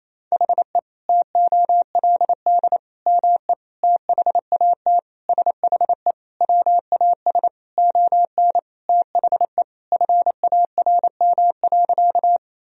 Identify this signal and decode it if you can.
Morse code, heard as a keyed tone, about 28 wpm, 700 Hz. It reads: HE TOLD ME THAT SHE WAS ON THE FARM.
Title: Sentences from Top 1000 Words - Review Part 2 - 28wpm